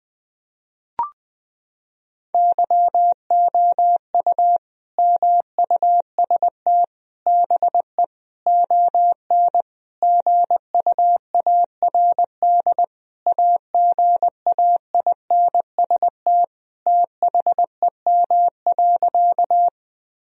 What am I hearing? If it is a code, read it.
Morse code, heard as a keyed tone, about 20 wpm, 700 Hz. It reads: YOU MUST BE ON GUARD AGAINST THEM.
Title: Sentences from Top 1000 Words - Review Part 4 - 20wpm